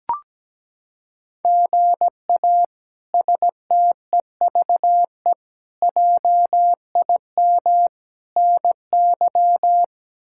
Morse code, heard as a keyed tone, about 17 wpm, 700 Hz. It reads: GA STEVE JIM NY
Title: K1USN SST QSO Practice - 17wpm